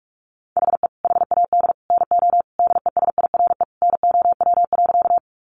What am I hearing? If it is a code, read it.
Morse code, heard as a keyed tone, about 44 wpm, 700 Hz. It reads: HE HAD NO DESIRE NOW.
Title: Sentences from Top 1100 Words 45wpm